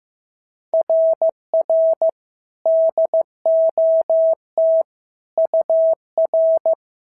Morse code, heard as a keyed tone, about 15 wpm, 650 Hz. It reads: RR DOT UR
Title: SOTA QSO Practice 15wpm